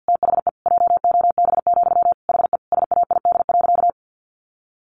Morse code, heard as a keyed tone, about 50 wpm, 700 Hz. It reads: THE JOB, HE SAID.
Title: Sentences from Top 1500 Words 50wpm